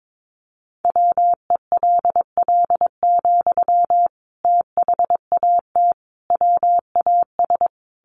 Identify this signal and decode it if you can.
Morse code, heard as a keyed tone, about 22 wpm, 700 Hz. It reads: WELL, THAT WAS
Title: Sentences from Top 1200 Words 22wpm